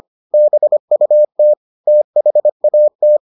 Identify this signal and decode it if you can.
Morse code, heard as a keyed tone, about 25 wpm, 600 Hz. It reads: BUT THAT